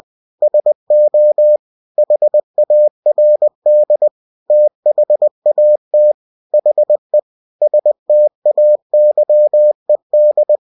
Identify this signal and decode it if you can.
Morse code, heard as a keyed tone, about 20 wpm, 600 Hz. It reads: SO HARD THAT HE STAYED
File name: cw-books / winnie-the-pooh / winnie-20wpm-page-0095